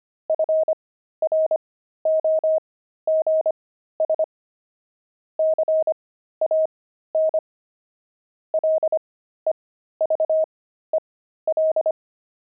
Morse code, heard as a keyed tone, about 25 wpm, spaced as for 13 wpm, 630 Hz. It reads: FROGS CAN LEVEL
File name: pangram-fs-13-cs-25-frogs